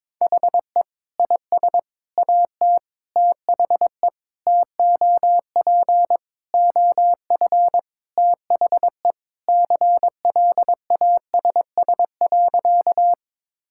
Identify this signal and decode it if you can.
Morse code, heard as a keyed tone, about 21 wpm, 700 Hz. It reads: HE IS AT THE TOP OF THE CLASS.
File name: practice / 22wpm / Sentences with 3-Word Phrases - 22wpm